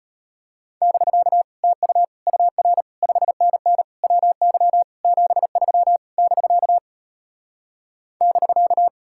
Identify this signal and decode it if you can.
Morse code, heard as a keyed tone, about 33 wpm, 700 Hz. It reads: <BK> TU UR 5NN WY 73 <BK> <BK>